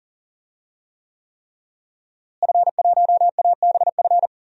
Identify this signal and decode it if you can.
Morse code, heard as a keyed tone, about 40 wpm, 700 Hz. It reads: F1ABF
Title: DXCC-100 Call Signs - 5 Characters - Repeated 3x - 40wpm